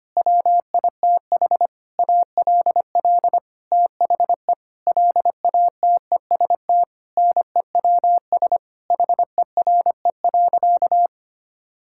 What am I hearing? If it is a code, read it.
Morse code, heard as a keyed tone, about 25 wpm, 700 Hz. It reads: WITH ALL THE LATEST NEWS HERE.
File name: practice / 25wpm / Sentences with 9-Word Phrases - 25wpm